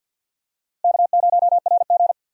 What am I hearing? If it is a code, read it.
Morse code, heard as a keyed tone, about 47 wpm, 700 Hz. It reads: K0RG
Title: USA Call Signs - 4 Characters - Repeated 2x - 50wpm